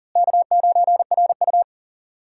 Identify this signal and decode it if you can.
Morse code, heard as a keyed tone, about 40 wpm, 700 Hz. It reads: K9RU
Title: USA Call Signs - 4 Characters - Repeated 2x - 40wpm